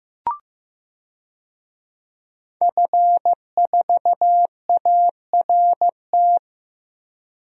Morse code, heard as a keyed tone, about 15 wpm, 700 Hz. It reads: F4ART
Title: DXCC-100 Call Signs - 5 Characters - Repeated 2x - 15wpm